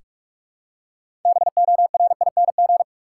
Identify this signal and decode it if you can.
Morse code, heard as a keyed tone, about 46 wpm, 700 Hz. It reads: BORING
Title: Six Letter Words - 45wpm